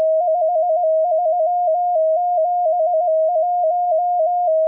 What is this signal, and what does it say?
Morse code, heard as a keyed tone, about 17 wpm, 710 Hz. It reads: 5371